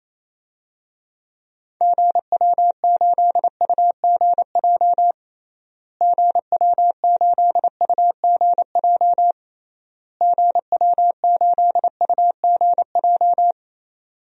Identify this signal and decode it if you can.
Morse code, heard as a keyed tone, about 27 wpm, 700 Hz. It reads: GW8UGJ GW8UGJ GW8UGJ
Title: DXCC-100 Call Signs - 6 Characters - Repeated 3x - 28wpm